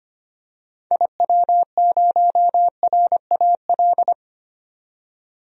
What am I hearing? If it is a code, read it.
Morse code, heard as a keyed tone, about 25 wpm, 700 Hz. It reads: IW0RAL
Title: DXCC-100 Call Signs - 6 Characters - 25wpm